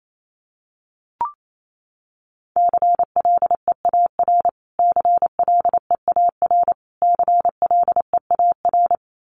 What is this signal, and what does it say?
Morse code, heard as a keyed tone, about 28 wpm, 700 Hz. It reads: CLEAR CLEAR CLEAR
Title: Top 500 Words Repeated 28wpm